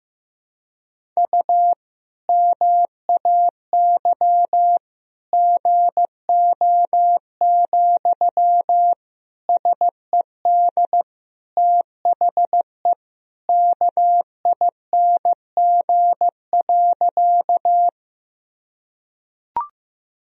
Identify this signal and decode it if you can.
Morse code, heard as a keyed tone, about 15 wpm, 700 Hz. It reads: U MAY GO, SED THE KING.
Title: Sentences from Top 400 Words and Abbreviations 15wpm